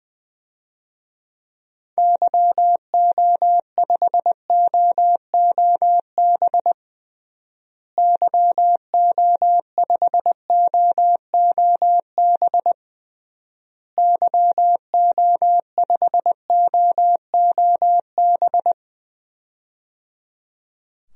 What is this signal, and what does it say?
Morse code, heard as a keyed tone, about 20 wpm, 700 Hz. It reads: YO5OOB YO5OOB YO5OOB
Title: DXCC-100 Call Signs - 6 Characters - Repeated 3x - 20wpm